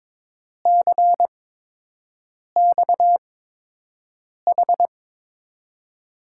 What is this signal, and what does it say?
Morse code, heard as a keyed tone, about 22 wpm, spaced as for 6 wpm, 700 Hz. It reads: CXH
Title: USA Call Signs - 6 Characters - 8x Character Spacing - 22wpm